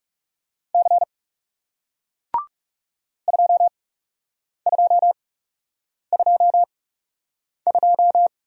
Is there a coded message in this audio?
Yes: Morse code from 0.7 to 8.3 s, heard as a keyed tone, about 39 wpm, spaced as for 8 wpm, 700 Hz. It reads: C 2222